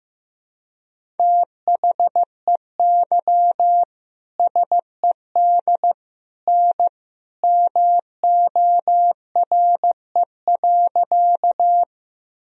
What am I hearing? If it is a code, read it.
Morse code, heard as a keyed tone, about 15 wpm, 700 Hz. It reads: THEY SED N MORE.
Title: Sentences from Top 100 Words and Abbreviations 15wpm